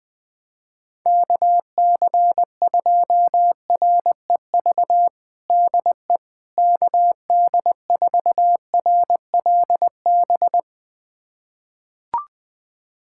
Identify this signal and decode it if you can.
Morse code, heard as a keyed tone, about 20 wpm, 700 Hz. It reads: KC2REV DE KD4RLB
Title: USA Callsign DE Callsign - 6 Characters - 20wpm